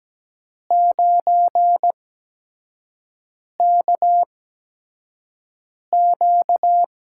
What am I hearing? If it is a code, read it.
Morse code, heard as a keyed tone, about 17 wpm, spaced as for 5 wpm, 700 Hz. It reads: 9KQ
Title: USA Call Signs - 4 Characters - 8x Character Spacing - 17wpm